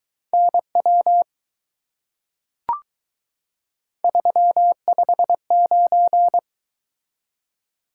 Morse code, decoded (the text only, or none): NW 359